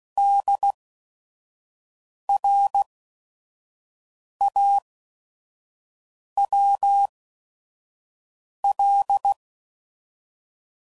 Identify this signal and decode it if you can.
Morse code, heard as a keyed tone, about 16 wpm, spaced as for 5 wpm, 800 Hz. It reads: DRAWL
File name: Prefixes / By First Letter / Ds